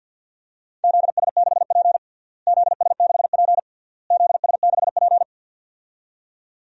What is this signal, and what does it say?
Morse code, heard as a keyed tone, about 50 wpm, 700 Hz. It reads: ZS6P ZS6P ZS6P
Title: DXCC-100 Call Signs - 4 Characters - Repeated 3x - 50wpm